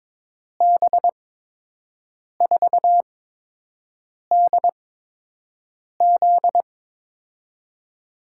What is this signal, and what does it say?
Morse code, heard as a keyed tone, about 22 wpm, spaced as for 6 wpm, 700 Hz. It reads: B4DZ
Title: USA Call Signs - 5 Characters - 8x Character Spacing - 22wpm